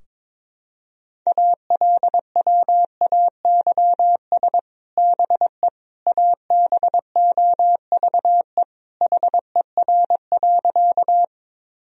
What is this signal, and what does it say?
Morse code, heard as a keyed tone, about 22 wpm, 700 Hz. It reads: ALWAYS BE ABOVE HER.
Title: Sentences from Top 1000 Words - Review Part 2 - 22wpm